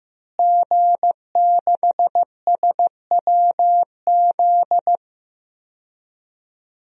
Morse code, heard as a keyed tone, about 15 wpm, 700 Hz. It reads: G6SWZ
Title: DXCC-100 Call Signs - 5 Characters - Repeated 2x - 15wpm